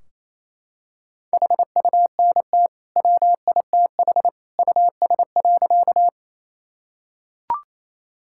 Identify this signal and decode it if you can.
Morse code, heard as a keyed tone, about 28 wpm, 700 Hz. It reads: HUNT WITH US.
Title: Sentences from Top 1600 Words 28wpm